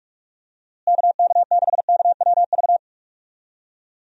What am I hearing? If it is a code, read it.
Morse code, heard as a keyed tone, about 46 wpm, 700 Hz. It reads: KK6KWV
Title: USA Call Signs - 6 Characters - Repeated 3x - 45wpm